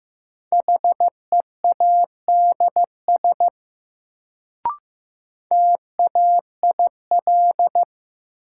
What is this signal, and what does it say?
Morse code, heard as a keyed tone, about 15 wpm, 700 Hz. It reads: HEADS TAIL
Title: Sentences from Top 1000 Words - Review Part 3 - New Word Drill - 15wpm